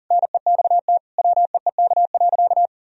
Morse code, heard as a keyed tone, about 40 wpm, 700 Hz. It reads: NEXT WEEK.
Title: Sentences from Top 1000 Words - Review Part 4 - 40wpm